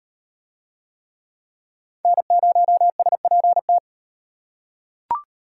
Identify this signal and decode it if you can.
Morse code, heard as a keyed tone, about 38 wpm, 700 Hz. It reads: N0SPT